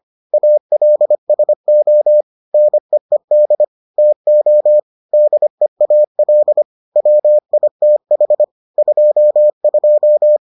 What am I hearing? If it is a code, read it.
Morse code, heard as a keyed tone, about 25 wpm, 600 Hz. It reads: ALSO NEED TO DEAL WITH 22